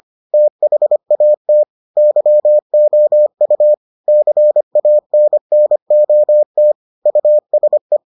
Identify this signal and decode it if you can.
Morse code, heard as a keyed tone, about 25 wpm, 600 Hz. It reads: THAT YOU CANNOT USE